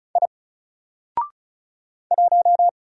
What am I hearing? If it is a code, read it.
Morse code, heard as a keyed tone, about 36 wpm, 700 Hz. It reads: I 1
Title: Single Letter-Number Alt 35wpm